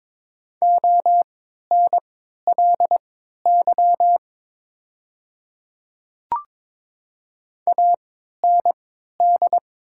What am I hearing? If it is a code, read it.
Morse code, heard as a keyed tone, about 22 wpm, spaced as for 12 wpm, 700 Hz. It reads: ONLY AND
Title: 3x Character Spacing - Top 200 Words - 22wpm